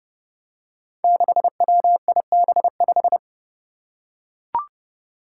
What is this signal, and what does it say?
Morse code, heard as a keyed tone, about 31 wpm, 700 Hz. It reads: 6WIB5
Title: Five Challenging Letter-Number Combinations 30wpm